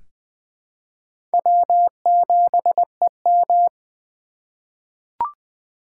Morse code, heard as a keyed tone, about 20 wpm, 700 Hz. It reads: W7EM
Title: USA Call Signs - 4 Characters - 2x Character Spacing - 20wpm